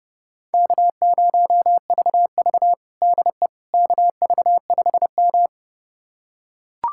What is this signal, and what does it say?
Morse code, heard as a keyed tone, about 32 wpm, 700 Hz. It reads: K0VV DE KV5M
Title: USA Callsign DE Callsign - 4 Characters - 30wpm